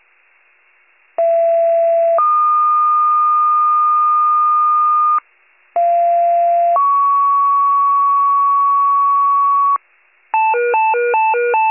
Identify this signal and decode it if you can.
Two-tone page, heard as a one-second tone, then a three-second tone, about 670 then 1220 Hz, about 690 then 1120 Hz. It is heard from 1.2 to 9.8 s.